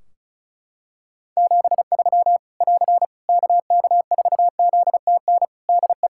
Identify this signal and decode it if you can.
Morse code, heard as a keyed tone, about 36 wpm, 700 Hz. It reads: 73 <AR> KK4ZTN DE
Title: Practice QSO - Exchange 4 - Ending - 35wpm